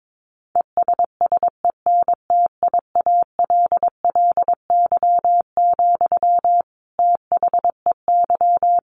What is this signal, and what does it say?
Morse code, heard as a keyed tone, about 22 wpm, 700 Hz. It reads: ESSENTIALLY, THEY
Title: Sentences from Top 1300 Words 22wpm